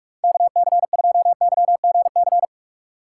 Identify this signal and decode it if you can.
Morse code, heard as a keyed tone, about 42 wpm, 700 Hz. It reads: KC2YGC
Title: USA Call Signs - 6 Characters - 2x Character Spacing - 45wpm